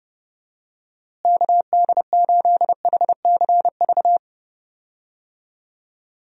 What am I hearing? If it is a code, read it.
Morse code, heard as a keyed tone, about 30 wpm, 700 Hz. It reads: KD8HCV
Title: USA Call Signs - 6 Characters - Repeated 2x - 30wpm